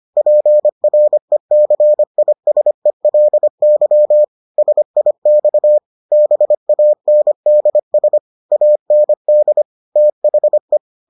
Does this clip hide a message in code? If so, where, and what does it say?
Yes: Morse code from 0.2 to 10.8 s, heard as a keyed tone, about 25 wpm, 600 Hz. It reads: PRECISELY SIX BANDS AND THE